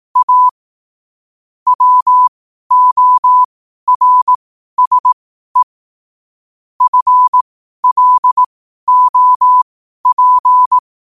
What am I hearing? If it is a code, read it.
Morse code, heard as a keyed tone, about 18 wpm, spaced as for 12 wpm, 1000 Hz. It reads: A WORSE FLOP
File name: qotdfiles / 12wpm